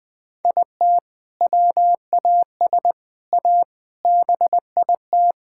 Morse code, heard as a keyed tone, about 20 wpm, 700 Hz. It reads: IT WAS A BIT
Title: Sets of 6 Words - 20wpm